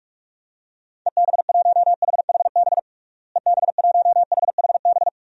Morse code, heard as a keyed tone, about 46 wpm, 700 Hz. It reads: EB1HHB EB1HHB
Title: DXCC-100 Call Signs - 6 Characters - Repeated 2x - 45wpm